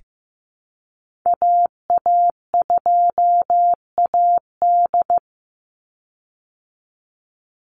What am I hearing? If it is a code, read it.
Morse code, heard as a keyed tone, about 15 wpm, 700 Hz. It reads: AA2AD
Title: Call Signs Repeated 2x - 15wpm